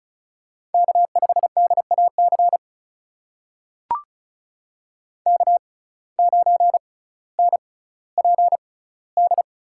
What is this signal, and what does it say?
Morse code, heard as a keyed tone, about 36 wpm, spaced as for 18 wpm, 700 Hz. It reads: K5DAC K9NPD
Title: USA Call Signs - 5 Characters - 6x Character Spacing - 35wpm